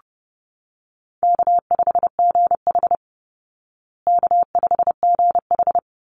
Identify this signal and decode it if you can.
Morse code, heard as a keyed tone, about 30 wpm, 700 Hz. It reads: K5GH K5GH
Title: Call Signs Repeated 3x - 30wpm